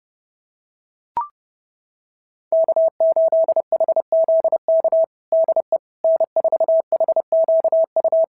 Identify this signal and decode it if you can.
Morse code, heard as a keyed tone, about 29 wpm, 690 Hz. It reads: K8HZK DE N4HQU
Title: SOTA QSO Practice 30wpm